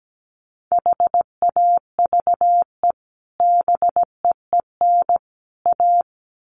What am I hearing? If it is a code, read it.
Morse code, heard as a keyed tone, about 17 wpm, 700 Hz. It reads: HAVE BEEN A
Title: Sets of 3 Words 17wpm